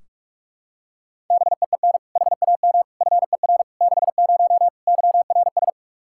Morse code, heard as a keyed tone, about 46 wpm, 700 Hz. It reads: BEEN HAM FER 60 YRS
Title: Practice QSO - Exchange 3 - More Info - 45wpm